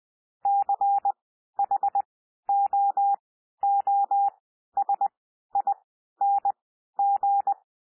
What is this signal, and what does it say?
Morse code, heard as a keyed tone, about 20 wpm, spaced as for 12 wpm, 800 Hz. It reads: CHOOSING
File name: cwmp3 / P1 SOUND PASSAGE 20 wpm